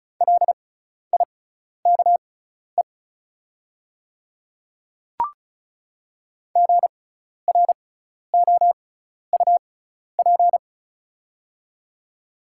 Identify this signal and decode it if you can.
Morse code, heard as a keyed tone, about 36 wpm, spaced as for 12 wpm, 690 Hz. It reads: LIKE GROUP
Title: 6x Character Spacing - Top 300 Words - 35wpm